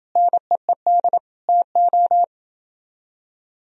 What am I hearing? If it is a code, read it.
Morse code, heard as a keyed tone, about 28 wpm, 700 Hz. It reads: NEED TO